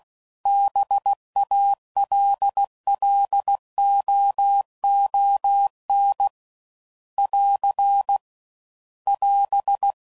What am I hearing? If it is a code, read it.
Morse code, heard as a keyed tone, about 16 wpm, 800 Hz. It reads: BALLOON <AR> <AS>